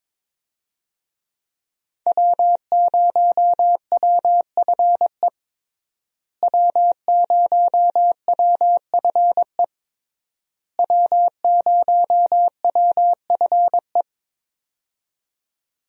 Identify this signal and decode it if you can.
Morse code, heard as a keyed tone, about 22 wpm, 700 Hz. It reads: W0WFE W0WFE W0WFE
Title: USA Call Signs - 5 Characters - Repeated 3x - 22wpm